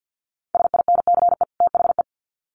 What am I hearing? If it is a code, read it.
Morse code, heard as a keyed tone, about 50 wpm, 700 Hz. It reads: SINCE THE
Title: Sets of 2 Words 50wpm